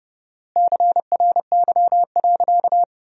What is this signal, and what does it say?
Morse code, heard as a keyed tone, about 30 wpm, 700 Hz. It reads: CRY.
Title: 4x Word-Spacing - Sentences from Top 500 Words - 30wpm